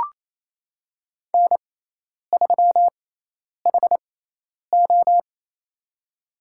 Morse code, heard as a keyed tone, about 28 wpm, spaced as for 10 wpm, 700 Hz. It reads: N3HO
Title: USA Call Signs - 4 Characters - 6x Character Spacing - 28wpm